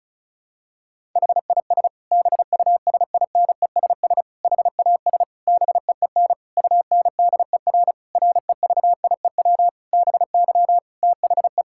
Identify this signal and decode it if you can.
Morse code, heard as a keyed tone, about 36 wpm, 700 Hz. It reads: HIS BUSINESS HAS BEEN UNDER REVIEW BY THE
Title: Sentences from Top 1000 Words - Review Part 6 - 35wpm